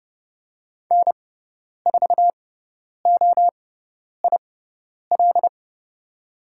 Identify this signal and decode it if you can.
Morse code, heard as a keyed tone, about 30 wpm, spaced as for 10 wpm, 700 Hz. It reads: N4OIL
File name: Lesson 16g - Letter c - Call Signs 30wpm